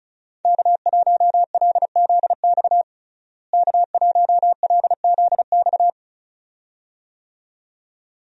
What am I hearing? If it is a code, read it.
Morse code, heard as a keyed tone, about 34 wpm, 700 Hz. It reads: K1LZX K1LZX
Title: USA Call Signs - 5 Characters - Repeated 2x - 35wpm